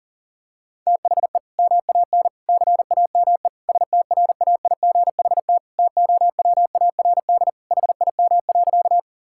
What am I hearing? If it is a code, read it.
Morse code, heard as a keyed tone, about 40 wpm, 700 Hz. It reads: THE MAN CAME STRAIGHT TOWARD HIM.
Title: Sentences from Top 1000 Words - Review Part 6 - 40wpm